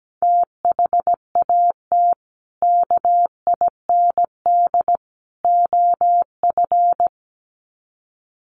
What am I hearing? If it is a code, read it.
Morse code, heard as a keyed tone, about 17 wpm, 700 Hz. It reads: THAT KIND OF
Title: Sets of 3 Words 17wpm